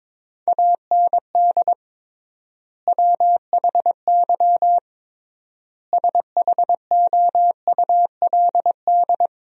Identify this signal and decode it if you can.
Morse code, heard as a keyed tone, about 22 wpm, 700 Hz. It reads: AND WHY SHOULD
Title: 3x Word-Spacing - Sentences from Top 300 Words - 22wpm